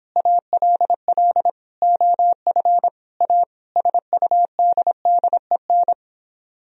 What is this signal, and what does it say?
Morse code, heard as a keyed tone, about 29 wpm, 700 Hz. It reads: ALL OF A SUDDEN